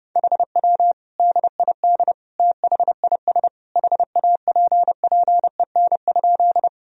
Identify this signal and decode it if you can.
Morse code, heard as a keyed tone, about 30 wpm, 700 Hz. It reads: HW DID THIS HAPPEN?
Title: Sentences from Top 400 Words and Abbreviations 30wpm